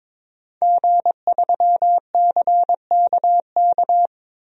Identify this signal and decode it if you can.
Morse code, heard as a keyed tone, about 22 wpm, 700 Hz. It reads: G3CKK
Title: DXCC-100 Call Signs - 5 Characters - Repeated 2x - 22wpm